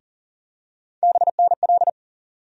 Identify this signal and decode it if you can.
Morse code, heard as a keyed tone, about 40 wpm, 700 Hz. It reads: BNL
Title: USA Call Sign Postfixes - 3 Letters 40wpm